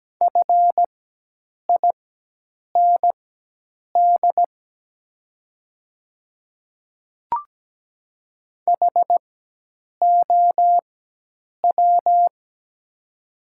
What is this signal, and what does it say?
Morse code, heard as a keyed tone, about 17 wpm, spaced as for 8 wpm, 700 Hz. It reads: FIND HOW
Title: 4x Character Spacing - Top 100 Words - 17wpm